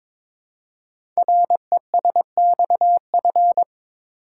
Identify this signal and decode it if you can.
Morse code, heard as a keyed tone, about 23 wpm, 700 Hz. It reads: RESXF
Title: Code Groups - 5 Letters - 22wpm